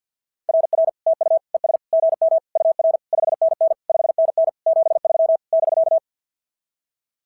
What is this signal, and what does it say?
Morse code, heard as a keyed tone, about 46 wpm, 660 Hz. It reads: RR TU ES GM UR 5NN 5NN 73 <BK>
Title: SOTA QSO Practice 50wpm